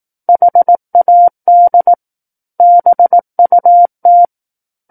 Morse code, heard as a keyed tone, about 18 wpm, 700 Hz. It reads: HAD BUT
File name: Book One / 09-Chapter 9 - 18 WPM